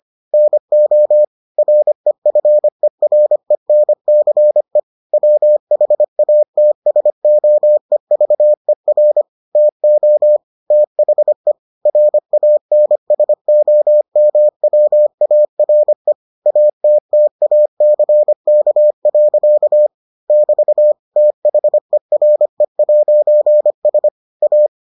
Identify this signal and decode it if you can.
Morse code, heard as a keyed tone, about 25 wpm, 600 Hz. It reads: NO REFERENCE WHATSOEVER TO THE RANSOMWARE ATTACK. <BT> THERE'S A